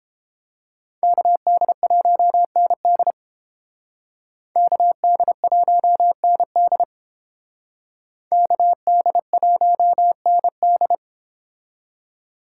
Morse code, heard as a keyed tone, about 30 wpm, 700 Hz. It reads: KD1ND KD1ND KD1ND